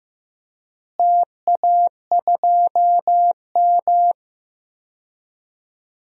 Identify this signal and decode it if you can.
Morse code, heard as a keyed tone, about 15 wpm, 700 Hz. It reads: TA2M
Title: DXCC-100 Call Signs - 4 Characters - Repeated 3x - 15wpm